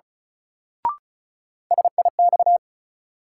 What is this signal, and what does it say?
Morse code, heard as a keyed tone, about 34 wpm, 700 Hz. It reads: SIX
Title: Sentences from Top 600 Words - New Word Drill - 35wpm